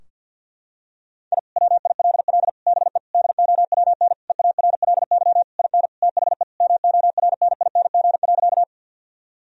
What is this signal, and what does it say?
Morse code, heard as a keyed tone, about 51 wpm, 700 Hz. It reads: I WILL BE DOWN EARLY IN THE MORNING.
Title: Sentences from Top 1000 Words - Review Part 2 - 50wpm